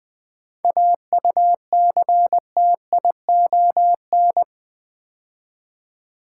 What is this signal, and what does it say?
Morse code, heard as a keyed tone, about 20 wpm, 700 Hz. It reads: AUCTION
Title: 7 Letter Words - First 3 Letters Given - 20wpm